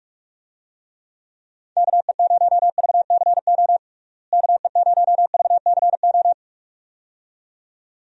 Morse code, heard as a keyed tone, about 45 wpm, 700 Hz. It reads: KE0VCO KE0VCO
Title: USA Call Signs - 6 Characters - Repeated 2x - 45wpm